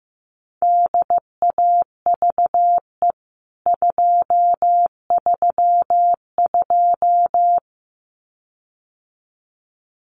Morse code, heard as a keyed tone, about 15 wpm, 700 Hz. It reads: DAVE 232
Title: CWT Exchange 15wpm